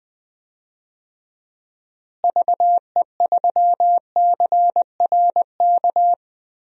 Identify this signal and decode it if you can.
Morse code, heard as a keyed tone, about 20 wpm, 700 Hz. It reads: VE3CRK
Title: DXCC-100 Call Signs - 6 Characters - Repeated 3x - 20wpm